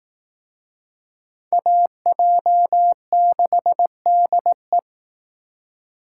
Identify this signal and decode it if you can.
Morse code, heard as a keyed tone, about 18 wpm, 700 Hz. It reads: AJ6DE
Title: Speed Racing - USA Call Signs 15wpm